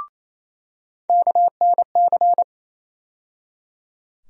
Morse code, heard as a keyed tone, about 28 wpm, 700 Hz. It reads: KNC
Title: USA Call Sign Postfixes - 3 Letters 28wpm